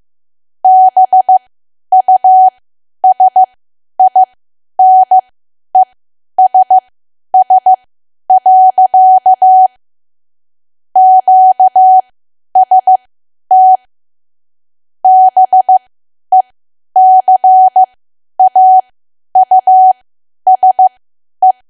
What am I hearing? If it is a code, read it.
Morse code, heard as a keyed tone, about 15 wpm, spaced as for 10 wpm, 750 Hz. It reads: BUSINESS. QST BECAUSE
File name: cwa / W1AW-Level2 / 10WPM